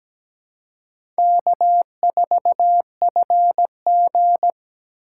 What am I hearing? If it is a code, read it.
Morse code, heard as a keyed tone, about 17 wpm, 700 Hz. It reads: K4FG
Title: USA Call Signs - 4 Characters - 4x Character Spacing - 17wpm